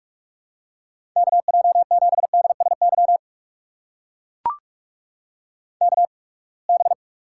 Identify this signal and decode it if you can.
Morse code, heard as a keyed tone, about 42 wpm, spaced as for 30 wpm, 700 Hz. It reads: KJ7DSY KB
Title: USA Call Signs - 6 Characters - 8x Character Spacing - 45wpm